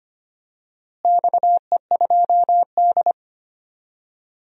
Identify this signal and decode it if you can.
Morse code, heard as a keyed tone, about 25 wpm, 700 Hz. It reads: XE2D
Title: DXCC-100 Call Signs - 4 Characters - Repeated 3x - 25wpm